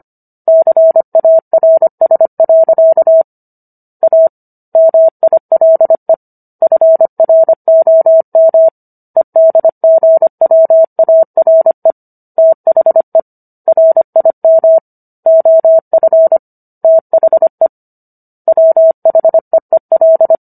Morse code, heard as a keyed tone, about 25 wpm, 650 Hz. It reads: CARS. A MILE FROM EDGWARE THE RIM OF THE WHEEL